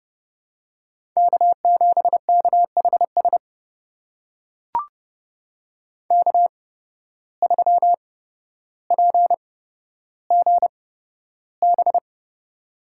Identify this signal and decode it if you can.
Morse code, heard as a keyed tone, about 29 wpm, spaced as for 13 wpm, 700 Hz. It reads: K7KHS K3PGB
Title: USA Call Signs - 5 Characters - 8x Character Spacing - 30wpm